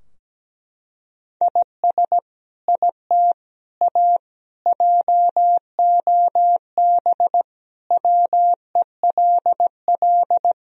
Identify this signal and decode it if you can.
Morse code, heard as a keyed tone, about 17 wpm, 700 Hz. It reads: IS IT A JOB WELL
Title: Sentences from Top 1000 Words - Review Part 3 - 17wpm